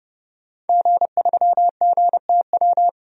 Morse code, heard as a keyed tone, about 31 wpm, 700 Hz. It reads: G3GTW
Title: DXCC-100 Call Signs - 5 Characters - 30wpm